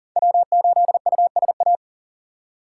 Morse code, heard as a keyed tone, about 51 wpm, 700 Hz. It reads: W8USA